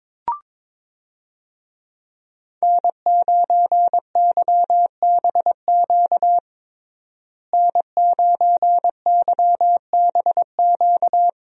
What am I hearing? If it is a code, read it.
Morse code, heard as a keyed tone, about 22 wpm, 700 Hz. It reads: N9YBQ N9YBQ
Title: USA Call Signs - 5 Characters - Repeated 3x - 22wpm